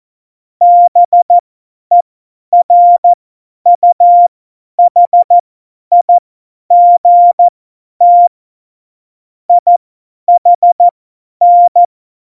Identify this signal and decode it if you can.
Morse code, heard as a keyed tone, about 14 wpm, spaced as for 10 wpm, 700 Hz. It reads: BERUHIGT IHN
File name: Morsekurs / Witze / Witz 08. Januar 2019, 50 F70